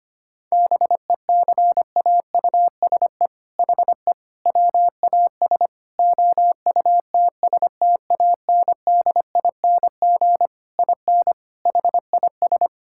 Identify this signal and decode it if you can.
Morse code, heard as a keyed tone, about 25 wpm, 700 Hz. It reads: BECAUSE HE WAS OUTSTANDING IN HIS
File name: practice / 25wpm / All-Age Appropriate Jokes - 25wpm